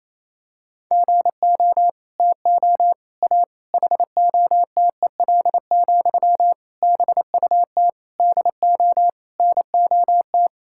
Morse code, heard as a keyed tone, about 28 wpm, 700 Hz. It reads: GO TO A HOTEL, BUT DO NOT
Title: Sentences from Top 1600 Words 28wpm